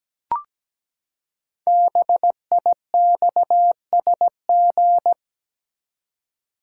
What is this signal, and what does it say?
Morse code, heard as a keyed tone, about 17 wpm, 700 Hz. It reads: BIXSG
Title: Code Groups - 5 Letters - 17wpm